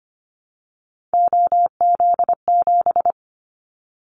Morse code, heard as a keyed tone, about 25 wpm, 700 Hz. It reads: OZ7
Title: Callsign Prefixes for Top 100 DXCC - 25wpm